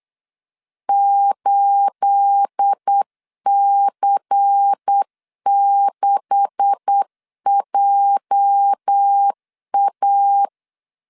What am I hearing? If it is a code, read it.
Morse code, heard as a keyed tone, about 8 wpm, 800 Hz. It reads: 8C6JA
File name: LEZIONI / 10 SIGNS / 10D COMA 8wpm 3749